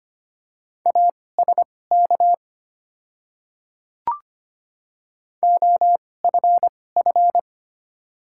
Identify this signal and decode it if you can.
Morse code, heard as a keyed tone, about 25 wpm, spaced as for 18 wpm, 700 Hz. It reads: ASK OFF
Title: 2x Character Spacing - Top 300 Words - 25wpm